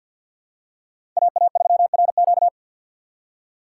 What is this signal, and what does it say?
Morse code, heard as a keyed tone, about 53 wpm, 700 Hz. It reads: AA3RQ
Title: USA Call Signs - 5 Characters - 6x Character Spacing - 50wpm